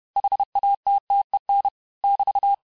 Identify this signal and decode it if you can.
Morse code, heard as a keyed tone, about 31 wpm, 800 Hz. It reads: HATTEN <BT>
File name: cours-du-mercredi / audios / cours-861-texte-les-mots-vitesse-29MPM